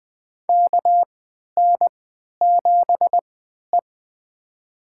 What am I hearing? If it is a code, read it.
Morse code, heard as a keyed tone, about 20 wpm, spaced as for 11 wpm, 700 Hz. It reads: KN7E
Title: USA Call Signs - 4 Characters - 3x Character Spacing - 20wpm